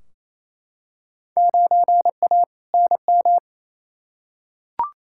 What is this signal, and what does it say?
Morse code, heard as a keyed tone, about 28 wpm, 700 Hz. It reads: 9A NM
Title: Field Day Exchanges Encore - 28wpm